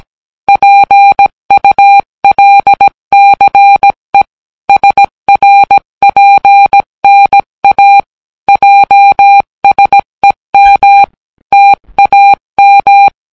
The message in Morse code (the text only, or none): PULCE SRPNA JSEM TAM